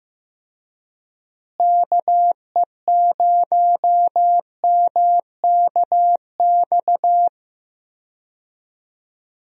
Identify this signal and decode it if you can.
Morse code, heard as a keyed tone, about 15 wpm, 700 Hz. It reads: KE0MKX
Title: USA Call Signs - 6 Characters - Repeated 3x - 15wpm